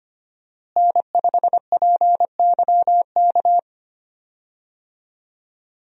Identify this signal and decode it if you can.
Morse code, heard as a keyed tone, about 25 wpm, 700 Hz. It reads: N5PYK
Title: USA Call Signs - 5 Characters - Repeated 2x - 25wpm